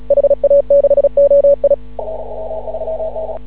Morse code, heard as a keyed tone, about 36 wpm, 560 Hz. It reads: HA6OI
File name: CW / ha6oi 2